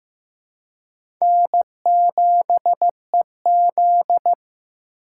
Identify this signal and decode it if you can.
Morse code, heard as a keyed tone, about 15 wpm, 700 Hz. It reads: N7EZ
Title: USA Call Signs - 4 Characters - 4x Character Spacing - 15wpm